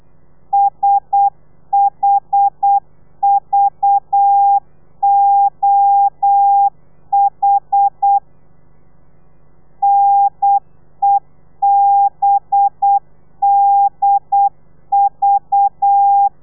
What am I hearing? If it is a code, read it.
Morse code, heard as a keyed tone, about 8 wpm, 800 Hz. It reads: SHVOH NEBDV
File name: telegrafia / i2tfj / Lez4 35